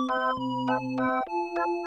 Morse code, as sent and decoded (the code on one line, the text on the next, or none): none